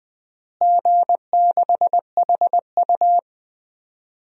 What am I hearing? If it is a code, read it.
Morse code, heard as a keyed tone, about 20 wpm, 700 Hz. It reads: G6HU
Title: DXCC-100 Call Signs - 4 Characters - Repeated 2x - 20wpm